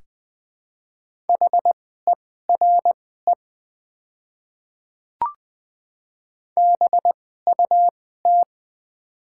Morse code, heard as a keyed tone, about 20 wpm, spaced as for 14 wpm, 700 Hz. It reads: HERE BUT
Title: 2x Character Spacing - Top 300 Words - 20wpm